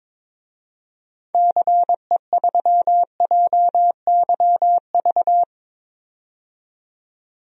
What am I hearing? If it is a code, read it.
Morse code, heard as a keyed tone, about 22 wpm, 700 Hz. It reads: CE3JYV
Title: DXCC-100 Call Signs - 6 Characters - 22wpm